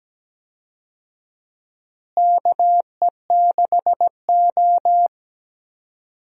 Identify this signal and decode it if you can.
Morse code, heard as a keyed tone, about 17 wpm, 700 Hz. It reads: KE6O
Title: USA Call Signs - 4 Characters - Repeated 3x - 17wpm